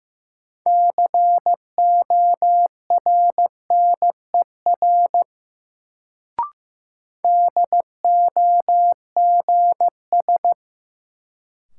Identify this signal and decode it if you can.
Morse code, heard as a keyed tone, about 15 wpm, 700 Hz. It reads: CORNER DOGS
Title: Sentences from Top 1600 Words - New Word Drill - 15wpm